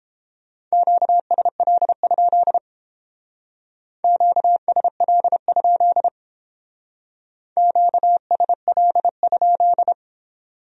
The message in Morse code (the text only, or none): QSL? QSL? QSL?